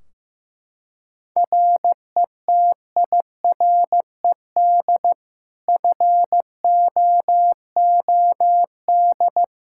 Morse code, heard as a keyed tone, about 15 wpm, 700 Hz. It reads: RETIRED FOOD
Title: Practice QSO - Exchange 3 - More Info - 15wpm